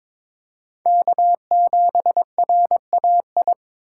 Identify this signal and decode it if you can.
Morse code, heard as a keyed tone, about 26 wpm, 700 Hz. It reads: K7RAI
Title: Speed Racing - USA 1x3 Call Signs 22wpm